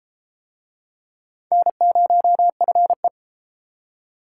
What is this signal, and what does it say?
Morse code, heard as a keyed tone, about 31 wpm, 700 Hz. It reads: N0FE